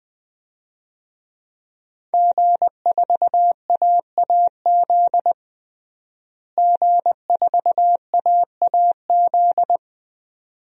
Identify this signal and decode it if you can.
Morse code, heard as a keyed tone, about 20 wpm, 700 Hz. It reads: G4AAZ G4AAZ
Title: DXCC-100 Call Signs - 5 Characters - Repeated 2x - 20wpm